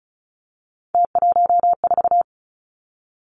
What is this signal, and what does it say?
Morse code, heard as a keyed tone, about 36 wpm, 700 Hz. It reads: T14
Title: Three Challenging Letter-Number Combinations 35wpm